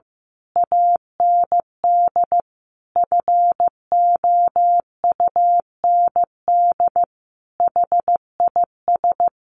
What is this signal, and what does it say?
Morse code, heard as a keyed tone, about 15 wpm, 700 Hz. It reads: AND FOUND HIS
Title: A Sound Of Thunder 15wpm